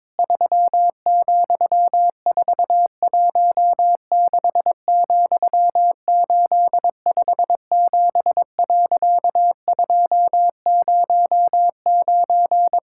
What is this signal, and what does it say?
Morse code, heard as a keyed tone, about 22 wpm, 700 Hz. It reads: 3,416,857.209
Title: Pangram Challenge v2 - 22wpm